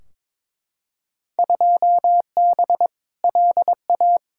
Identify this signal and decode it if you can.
Morse code, heard as a keyed tone, about 22 wpm, 700 Hz. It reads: 2B LA
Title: Field Day Exchanges Encore - 22wpm